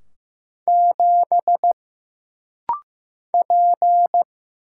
Morse code, heard as a keyed tone, about 15 wpm, 700 Hz. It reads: 7 P
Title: Single Letter-Number Alt - Rapid-Fire 15wpm